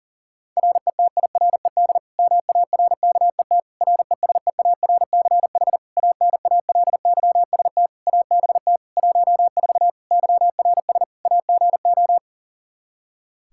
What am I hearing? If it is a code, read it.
Morse code, heard as a keyed tone, about 40 wpm, 700 Hz. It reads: RETIRED MARKET RESEARCH ANALYST ABT 14 YRS AGO
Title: Practice QSO - Exchange 3 - More Info - 40wpm